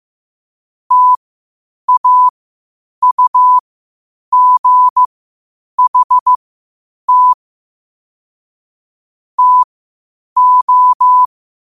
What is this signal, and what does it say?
Morse code, heard as a keyed tone, about 15 wpm, spaced as for 8 wpm, 1000 Hz. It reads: TAUGHT TO
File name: qotdfiles / 8wpm